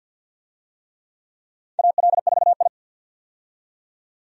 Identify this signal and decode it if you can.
Morse code, heard as a keyed tone, about 49 wpm, 700 Hz. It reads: AL4I
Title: USA Call Signs - 4 Characters - 50wpm